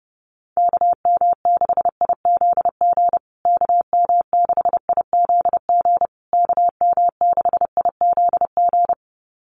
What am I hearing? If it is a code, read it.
Morse code, heard as a keyed tone, about 30 wpm, 700 Hz. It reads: KM6IZG KM6IZG KM6IZG
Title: USA Call Signs Repeated 3x 30wpm